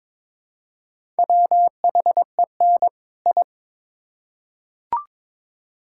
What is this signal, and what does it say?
Morse code, heard as a keyed tone, about 22 wpm, 700 Hz. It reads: WHEN I
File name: Build A Phrase - 2-Word Phrases - 22wpm